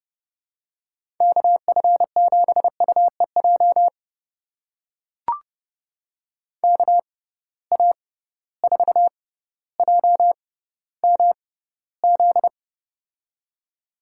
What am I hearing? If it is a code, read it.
Morse code, heard as a keyed tone, about 30 wpm, spaced as for 15 wpm, 700 Hz. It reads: KF7UEJ KA4JMZ
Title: USA Call Signs - 6 Characters - 6x Character Spacing - 30wpm